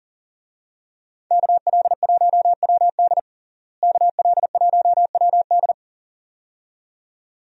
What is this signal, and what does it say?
Morse code, heard as a keyed tone, about 37 wpm, 700 Hz. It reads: KL1WD KL1WD
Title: USA Call Signs - 5 Characters - Repeated 2x - 40wpm